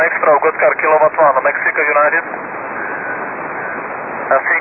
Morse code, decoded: none